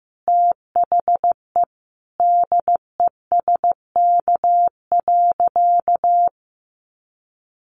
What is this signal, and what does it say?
Morse code, heard as a keyed tone, about 15 wpm, 700 Hz. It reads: THE DESK.
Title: A Sound Of Thunder 15wpm